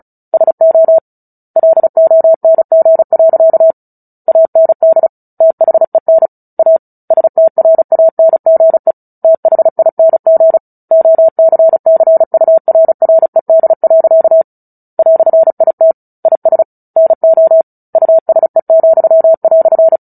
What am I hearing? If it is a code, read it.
Morse code, heard as a keyed tone, about 34 wpm, 650 Hz. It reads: SO LONG. AND THEN A STRANGE THING OCCURRED. "IT IS NO USE,"
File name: Book Two / 09-Chapter 9 - 35 WPM